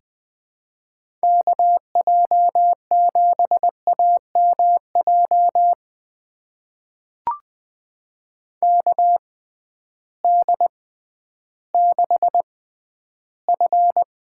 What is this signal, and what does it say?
Morse code, heard as a keyed tone, about 20 wpm, spaced as for 12 wpm, 700 Hz. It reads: KJ7AMJ KD6F
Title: USA Call Signs - 6 Characters - 6x Character Spacing - 20wpm